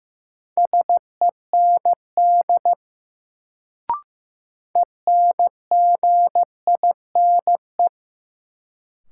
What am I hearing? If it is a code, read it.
Morse code, heard as a keyed tone, about 15 wpm, 700 Hz. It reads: SEND ENGINE
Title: Sentences from Top 1500 Words - New Word Drill - 15wpm